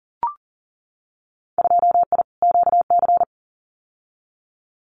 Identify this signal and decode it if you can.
Morse code, heard as a keyed tone, about 39 wpm, 700 Hz. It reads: 2I QC
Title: Winter Field Day Exchanges 40wpm